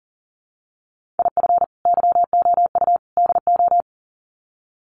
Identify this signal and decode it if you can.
Morse code, heard as a keyed tone, about 40 wpm, 700 Hz. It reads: IF YOU DO
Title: Sets of 3 Words 40wpm